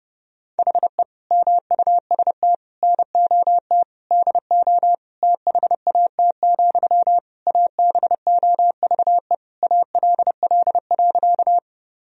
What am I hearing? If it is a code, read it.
Morse code, heard as a keyed tone, about 32 wpm, 700 Hz. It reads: HE MUST NOT DO THAT, ABOVE ALL.
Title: Sentences from Top 1000 Words - Review Part 2 - 30wpm